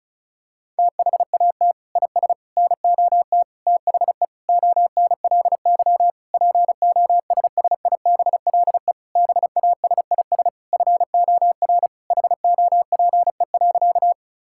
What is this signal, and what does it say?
Morse code, heard as a keyed tone, about 36 wpm, 700 Hz. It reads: THAT IS NOT THE ONLY POSSIBLE BASIS FOR HOPE.
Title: Sentences from Top 1000 Words - Review Part 5 - 35wpm